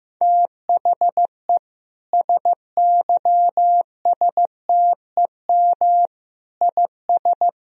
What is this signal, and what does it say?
Morse code, heard as a keyed tone, about 15 wpm, 700 Hz. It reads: THE SYSTEM IS
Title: Sentences from Top 1000 Words - Review Part 3 - 15wpm